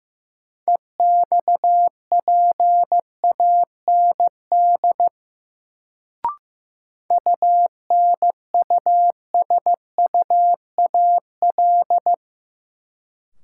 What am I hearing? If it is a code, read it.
Morse code, heard as a keyed tone, about 15 wpm, 700 Hz. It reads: EXPAND UNUSUAL
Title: Sentences from Top 1400 Words - New Word Drill Alt - 15wpm